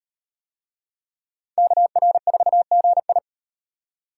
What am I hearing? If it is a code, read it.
Morse code, heard as a keyed tone, about 36 wpm, 700 Hz. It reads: KR4GI